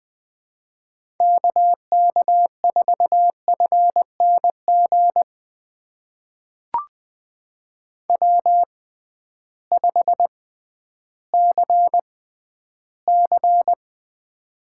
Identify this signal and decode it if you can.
Morse code, heard as a keyed tone, about 20 wpm, spaced as for 12 wpm, 700 Hz. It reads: KK4FNG W5CC
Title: USA Call Signs - 6x Character Spacing - 20wpm